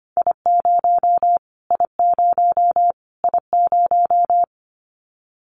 Morse code, heard as a keyed tone, about 25 wpm, 700 Hz. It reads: I0 I0 I0
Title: Callsign Prefixes for Top 25 DXCC - Repeated 3x - 25wpm